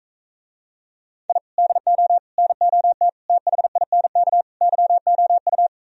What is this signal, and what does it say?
Morse code, heard as a keyed tone, about 45 wpm, 700 Hz. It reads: I DO NOT THINK YOU